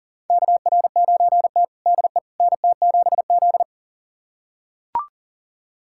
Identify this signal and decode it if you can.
Morse code, heard as a keyed tone, about 40 wpm, 700 Hz. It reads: KR9T DE NT7Z
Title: USA Callsign DE Callsign - 4 Characters - 40wpm